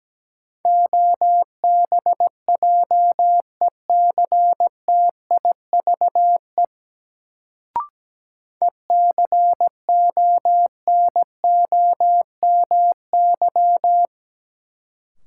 Morse code, heard as a keyed tone, about 17 wpm, 700 Hz. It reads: OBJECTIVE ECONOMY
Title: Sentences from Top 1200 Words - New Word Drill Alt - 17wpm